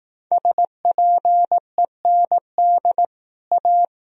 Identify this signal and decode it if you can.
Morse code, heard as a keyed tone, about 18 wpm, 700 Hz. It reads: SPEND A
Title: Speed-Racing - Sets of 5 Words 15wpm